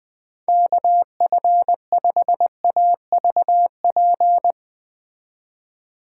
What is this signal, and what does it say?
Morse code, heard as a keyed tone, about 20 wpm, 700 Hz. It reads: KF5AVP
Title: Speed Racing - USA 2x3 Call Signs 17wpm